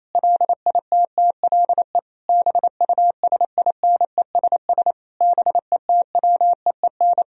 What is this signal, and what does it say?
Morse code, heard as a keyed tone, about 28 wpm, 700 Hz. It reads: LITTLE BUSINESS BETWEEN
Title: Sentences from Top 1000 Words - Review Part 1 - 28wpm